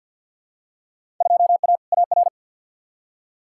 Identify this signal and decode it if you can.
Morse code, heard as a keyed tone, about 50 wpm, 700 Hz. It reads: 2A AR
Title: Field Day Exchanges - 50wpm